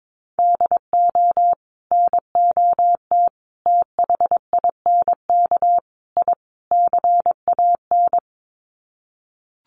Morse code, heard as a keyed tone, about 22 wpm, 700 Hz. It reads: DO NOT THINK I CAN
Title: Sets of 5 Words Encore2 22wpm